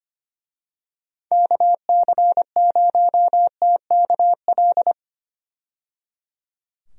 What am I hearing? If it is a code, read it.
Morse code, heard as a keyed tone, about 25 wpm, 700 Hz. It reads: KC0TKL
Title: USA Call Signs - 6 Characters - 25wpm